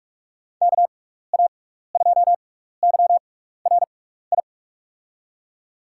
Morse code, heard as a keyed tone, about 44 wpm, spaced as for 15 wpm, 700 Hz. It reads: KA2YRI
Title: USA Call Signs - 6 Characters - 6x Character Spacing - 45wpm